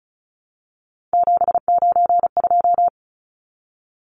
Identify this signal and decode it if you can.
Morse code, heard as a keyed tone, about 34 wpm, 700 Hz. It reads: 792
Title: Numbers - 3-Digit 35wpm